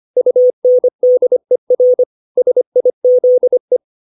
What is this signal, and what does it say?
Morse code, heard as a keyed tone, about 25 wpm, 500 Hz. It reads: UNDER SIZE